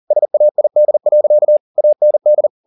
Morse code, heard as a keyed tone, about 40 wpm, 600 Hz. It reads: SAID. AND